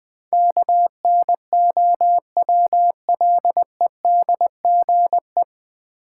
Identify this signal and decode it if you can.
Morse code, heard as a keyed tone, about 20 wpm, 700 Hz. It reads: KNOWLEDGE
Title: Sentences from Top 600 Words - New Word Drill - 20wpm